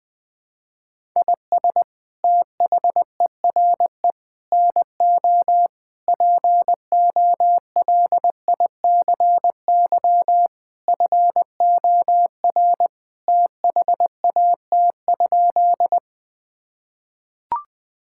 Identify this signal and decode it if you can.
Morse code, heard as a keyed tone, about 20 wpm, 700 Hz. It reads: IS THERE NO POLICY FOR THAT?
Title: Sentences from Top 1000 Words - Review Part 5 - 20wpm